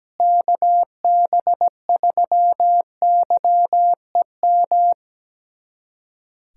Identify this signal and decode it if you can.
Morse code, heard as a keyed tone, about 17 wpm, 700 Hz. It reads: KB3YEM
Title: USA Callsign DE Callsign 17wpm